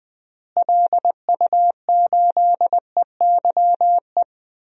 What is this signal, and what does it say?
Morse code, heard as a keyed tone, about 20 wpm, 700 Hz. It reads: LU8EYE
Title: DXCC-100 Call Signs - 6 Characters - Repeated 2x - 20wpm